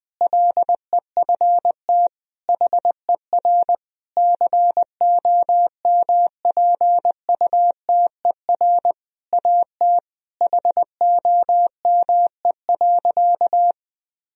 Morse code, heard as a keyed tone, about 21 wpm, 700 Hz. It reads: LEFT HER COMPUTER AT HOME.
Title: Sentences from Top 1000 Words - Review Part 3 - 20wpm